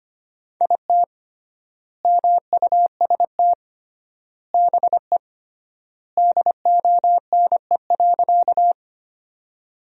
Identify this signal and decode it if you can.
Morse code, heard as a keyed tone, about 25 wpm, 700 Hz. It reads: IT MUST BE DONE.
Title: 3x Word-Spacing - Sentences from Top 500 Words - 25wpm